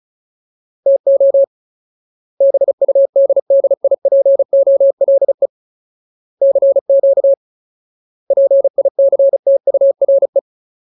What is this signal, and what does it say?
Morse code, heard as a keyed tone, about 34 wpm, 550 Hz. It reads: TO BUDDIPOLE CQ PICTURE
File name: Ultimate Word List / 35WPM Ultimate